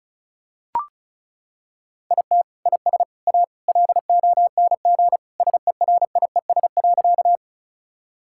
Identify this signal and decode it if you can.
Morse code, heard as a keyed tone, about 36 wpm, 700 Hz. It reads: IT IS A LONG SERIES.
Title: Sentences from Top 1000 Words - Review Part 5 - 35wpm